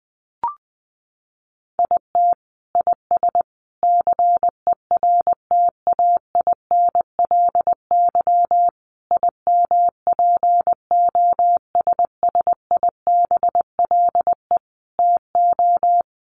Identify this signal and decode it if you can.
Morse code, heard as a keyed tone, about 19 wpm, 700 Hz. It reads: IT IS CERTAINLY IMPOSSIBLE TO
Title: Sentences from Top 1100 Words 20wpm